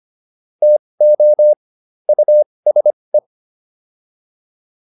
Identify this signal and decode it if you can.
Morse code, heard as a keyed tone, about 25 wpm, spaced as for 20 wpm, 600 Hz. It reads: TO USE